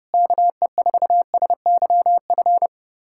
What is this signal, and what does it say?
Morse code, heard as a keyed tone, about 30 wpm, 700 Hz. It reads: KE4SYF
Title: USA Callsign DE Callsign - 6 Characters - 30wpm